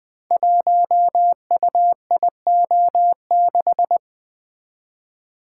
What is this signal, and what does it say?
Morse code, heard as a keyed tone, about 20 wpm, 700 Hz. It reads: 1UIO6
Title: Code Groups - 5 Letter:Numbers - 20wpm